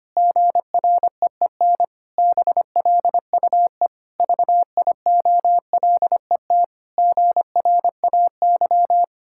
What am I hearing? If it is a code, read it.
Morse code, heard as a keyed tone, about 25 wpm, 700 Hz. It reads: GREEN BLUE VIOLET GRAY
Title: Pangram Challenge v2 - 25wpm